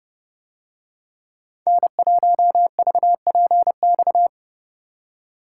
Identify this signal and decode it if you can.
Morse code, heard as a keyed tone, about 29 wpm, 700 Hz. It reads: N1VPX